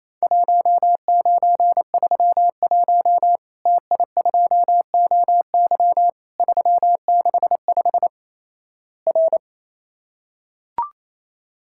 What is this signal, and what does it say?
Morse code, heard as a keyed tone, about 28 wpm, 700 Hz. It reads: 1931 TI2OY 365 R
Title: Worked All Europe DX Contest - QTC Exchange - 28wpm